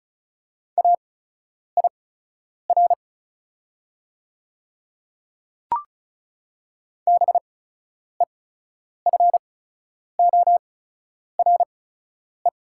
Morse code, heard as a keyed tone, about 37 wpm, spaced as for 10 wpm, 700 Hz. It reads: AIR BEFORE
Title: 8x Character Spacing - Top 300 Words - 35wpm